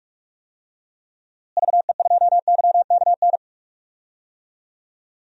Morse code, heard as a keyed tone, about 46 wpm, 700 Hz. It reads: VE2YKN
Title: DXCC-100 Call Signs - 6 Characters - Repeated 3x - 45wpm